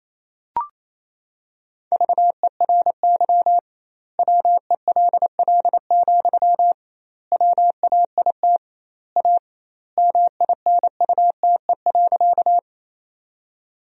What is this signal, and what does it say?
Morse code, heard as a keyed tone, about 28 wpm, 700 Hz. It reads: VERY WELL, WAIT A MINUTE.
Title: 2x Word-Spacing - Sentences from Top 500 Words - 28wpm